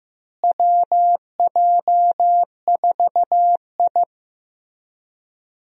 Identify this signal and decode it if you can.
Morse code, heard as a keyed tone, about 15 wpm, 700 Hz. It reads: WJ4I
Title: USA Call Signs - 4 Characters - Repeated 3x - 15wpm